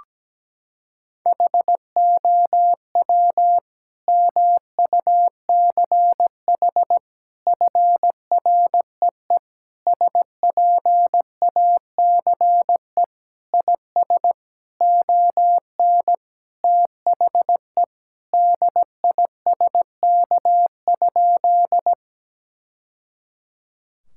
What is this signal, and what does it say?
Morse code, heard as a keyed tone, about 17 wpm, 700 Hz. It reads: HOW MUCH FREE SPACE IS ON THE DISK?
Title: Sentences from Top 1000 Words - Review Part 6 - 17wpm